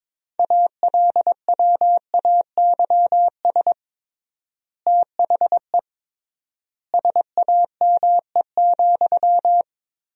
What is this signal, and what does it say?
Morse code, heard as a keyed tone, about 22 wpm, 700 Hz. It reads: ALWAYS THE SAME,
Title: 3x Word-Spacing - Sentences from Top 300 Words - 22wpm